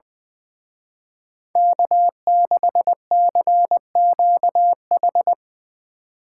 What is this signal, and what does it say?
Morse code, heard as a keyed tone, about 23 wpm, 700 Hz. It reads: K6CQH
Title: Speed Racing - USA 1x3 Call Signs 20wpm